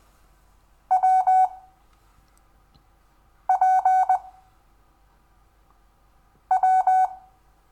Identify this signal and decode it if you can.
Morse code, heard as a keyed tone, about 20 wpm, 750 Hz. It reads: W P W
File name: cw / Lesson6 WI